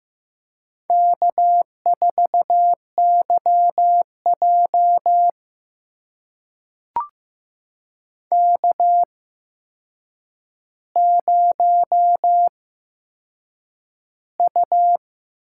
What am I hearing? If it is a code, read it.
Morse code, heard as a keyed tone, about 15 wpm, spaced as for 7 wpm, 700 Hz. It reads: K4YJ K0U
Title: USA Call Signs - 4 Characters - 8x Character Spacing - 15wpm